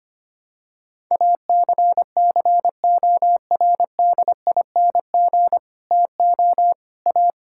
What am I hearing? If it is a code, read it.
Morse code, heard as a keyed tone, about 25 wpm, 700 Hz. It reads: ACCORDING TO A
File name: practice / 25wpm / Sentences with 9-Word Phrases - 25wpm